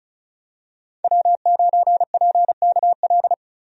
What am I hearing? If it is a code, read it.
Morse code, heard as a keyed tone, about 36 wpm, 700 Hz. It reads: W9PKL
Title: USA Call Signs - 5 Characters - 35wpm